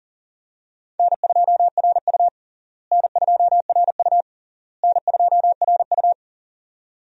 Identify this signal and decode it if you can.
Morse code, heard as a keyed tone, about 37 wpm, 700 Hz. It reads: N2RU N2RU N2RU
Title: USA Call Signs - 4 Characters - Repeated 3x - 40wpm